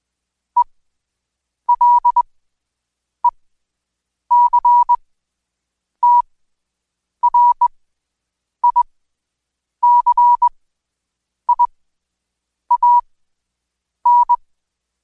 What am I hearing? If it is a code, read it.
Morse code, heard as a keyed tone, about 21 wpm, spaced as for 7 wpm, 970 Hz. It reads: ELECTRICIAN
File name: Code Quick / AudioDownld2 / Seg113-Day18-DayDrill 18